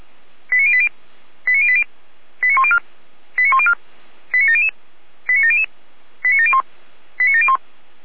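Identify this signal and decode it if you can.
Five-tone selective call (ZVEI), heard as quick runs of five tones, tones of about 70 ms. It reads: 89080 89080 89104 89104 89700 89700 89701 89701
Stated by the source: Five-tone ZVEI selective call received on an SDR.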